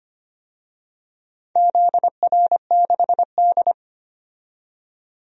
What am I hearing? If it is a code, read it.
Morse code, heard as a keyed tone, about 25 wpm, 700 Hz. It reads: ZR6D
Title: DXCC-100 Call Signs - 4 Characters - 25wpm